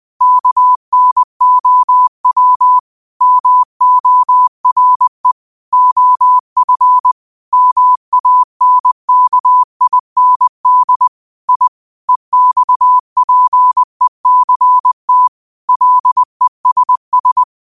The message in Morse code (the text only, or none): KNOW MORE OF MANKIND I EXPECT LESS